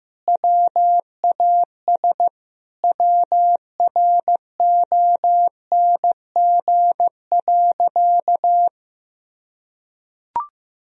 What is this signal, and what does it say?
Morse code, heard as a keyed tone, about 15 wpm, 700 Hz. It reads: WAS WRONG.
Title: Sentences from Top 1000 Words - Review Part 3 - 15wpm